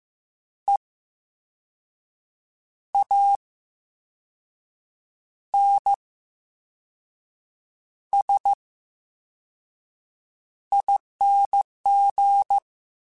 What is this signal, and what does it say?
Morse code, heard as a keyed tone, about 15 wpm, spaced as for 5 wpm, 800 Hz. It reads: EANSING
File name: Special INGs-mp3s / Special INGs-mp3s / VerySlowToFaster Ing @13